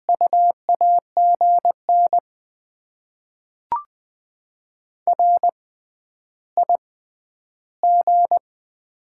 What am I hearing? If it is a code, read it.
Morse code, heard as a keyed tone, about 20 wpm, spaced as for 10 wpm, 690 Hz. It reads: U AGN RIG
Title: Top 20 QSO Elements - 6x Character Spacing - 20wpm